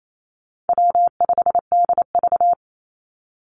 Morse code, heard as a keyed tone, about 28 wpm, 700 Hz. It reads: W5DV
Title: Call Signs Repeated 2x - 28wpm